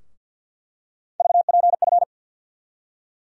Morse code, heard as a keyed tone, about 50 wpm, 700 Hz. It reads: VPF
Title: USA Call Sign Postfixes - 3 Letters 50wpm